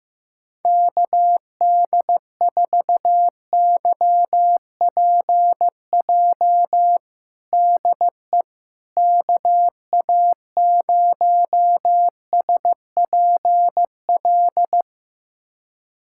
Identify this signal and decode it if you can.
Morse code, heard as a keyed tone, about 15 wpm, 700 Hz. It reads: KD4YPJ DE KA0SPL
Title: USA Callsign DE Callsign - 6 Characters - 15wpm